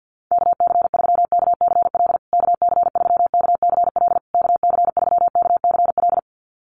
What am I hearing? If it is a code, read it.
Morse code, heard as a keyed tone, about 50 wpm, 700 Hz. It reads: KC3KCL KC3KCL KC3KCL
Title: USA Call Signs Repeated 3x 50wpm